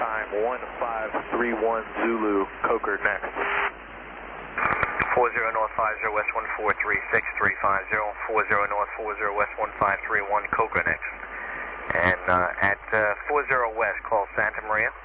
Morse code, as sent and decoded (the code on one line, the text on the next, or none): none